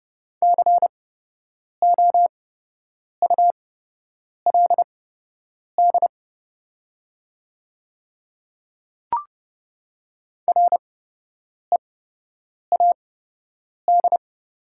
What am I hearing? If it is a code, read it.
Morse code, heard as a keyed tone, about 30 wpm, spaced as for 8 wpm, 700 Hz. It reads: COULD READ
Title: 8x Character Spacing - Top 200 Words - 30wpm